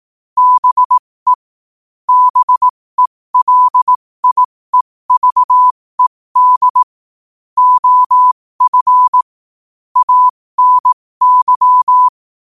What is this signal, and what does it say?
Morse code, heard as a keyed tone, about 18 wpm, spaced as for 15 wpm, 1000 Hz. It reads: BE BELIEVED OF ANY
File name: qotdfiles / 15wpm